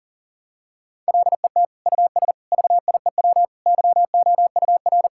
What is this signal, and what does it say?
Morse code, heard as a keyed tone, about 40 wpm, 700 Hz. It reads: LET US VIEW YOUR
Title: Sentences from Top 1000 Words - Review Part 6 - 40wpm